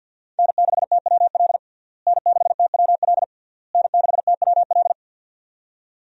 Morse code, heard as a keyed tone, about 46 wpm, 700 Hz. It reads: N6TWL N6TWL N6TWL
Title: USA Call Signs - 5 Characters - Repeated 3x - 50wpm